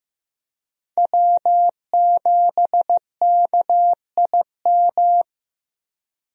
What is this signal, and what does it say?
Morse code, heard as a keyed tone, about 15 wpm, 700 Hz. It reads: W7KIM
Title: USA Call Signs - 5 Characters - 15wpm